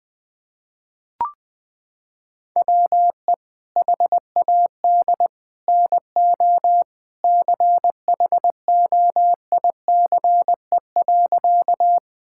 Morse code, heard as a keyed tone, about 21 wpm, 700 Hz. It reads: WE HAD NO CHOICE.
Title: Sentences from Top 1000 Words - Review Part 6 - 20wpm